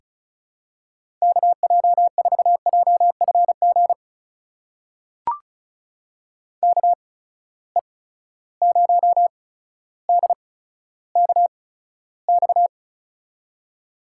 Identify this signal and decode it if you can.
Morse code, heard as a keyed tone, about 36 wpm, spaced as for 15 wpm, 700 Hz. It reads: KJ4JFG KE0DKX
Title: USA Call Signs - 6 Characters - 8x Character Spacing - 35wpm